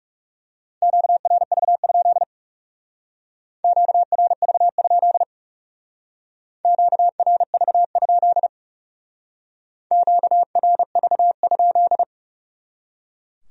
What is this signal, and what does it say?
Morse code, heard as a keyed tone, about 38 wpm, 700 Hz. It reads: QRV? QRV? QRV? QRV?